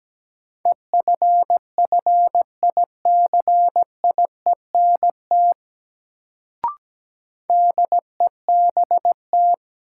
Morse code, heard as a keyed tone, about 17 wpm, 700 Hz. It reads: EFFICIENT DEBT
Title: Sentences from Top 1300 Words - New Word Drill - 17wpm